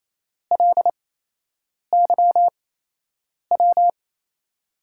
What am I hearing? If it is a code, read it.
Morse code, heard as a keyed tone, about 28 wpm, spaced as for 8 wpm, 700 Hz. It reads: LYW
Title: USA Call Signs - 5 Characters - 8x Character Spacing - 28wpm